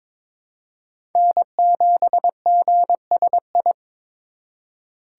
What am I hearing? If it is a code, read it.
Morse code, heard as a keyed tone, about 22 wpm, 700 Hz. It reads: N7GSI
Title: USA Call Signs - 5 Characters - 4x Character Spacing - 22wpm